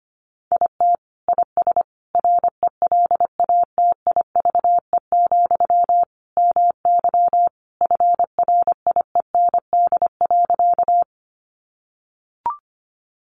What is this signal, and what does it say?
Morse code, heard as a keyed tone, about 25 wpm, 700 Hz. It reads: IT IS RELATIVE, MY FRIEND.
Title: Sentences from Top 1400 Words 25wpm